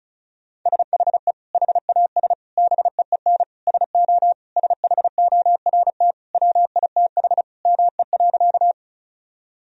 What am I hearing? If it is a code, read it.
Morse code, heard as a keyed tone, about 36 wpm, 700 Hz. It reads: SHE HAS BEEN SO SHORT WITH ME.
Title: Sentences from Top 1000 Words - Review Part 2 - 35wpm